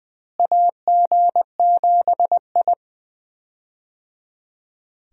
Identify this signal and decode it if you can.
Morse code, heard as a keyed tone, about 20 wpm, 700 Hz. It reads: AG7I
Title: USA Call Signs - 4 Characters - Repeated 3x - 20wpm